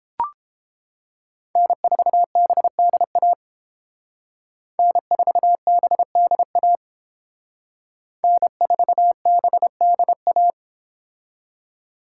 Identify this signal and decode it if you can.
Morse code, heard as a keyed tone, about 31 wpm, 700 Hz. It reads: N4BDA N4BDA N4BDA